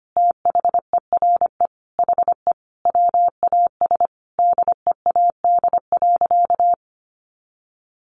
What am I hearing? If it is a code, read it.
Morse code, heard as a keyed tone, about 25 wpm, 700 Hz. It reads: THERE HE WAS DEAD.
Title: Sentences from Top 1500 Words 25wpm